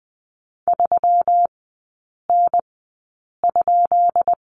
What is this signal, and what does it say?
Morse code, heard as a keyed tone, about 20 wpm, spaced as for 8 wpm, 700 Hz. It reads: 3N?
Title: 152 Lesson 25e - Letter p - Three Characters 20wpm